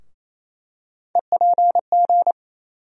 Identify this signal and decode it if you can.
Morse code, heard as a keyed tone, about 28 wpm, 700 Hz. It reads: EPG
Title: USA Call Sign Postfixes - 3 Letters 28wpm